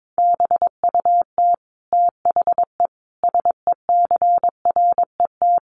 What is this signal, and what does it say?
Morse code, heard as a keyed tone, about 22 wpm, 700 Hz. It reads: BUT THE SECRET
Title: Sentences from Top 1200 Words 22wpm